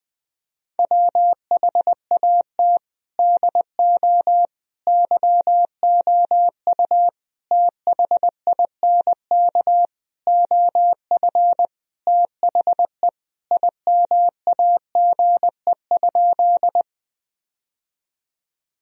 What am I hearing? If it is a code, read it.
Morse code, heard as a keyed tone, about 21 wpm, 700 Hz. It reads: WHAT DO YOU THINK OF THE IMAGE?
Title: Sentences from Top 1000 Words - Review Part 4 - 20wpm